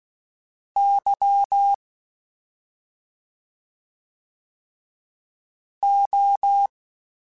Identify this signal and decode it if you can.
Morse code, heard as a keyed tone, about 16 wpm, spaced as for 2 wpm, 800 Hz. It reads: YO
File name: Special INGs-mp3s / Special INGs-mp3s / Slow ing @10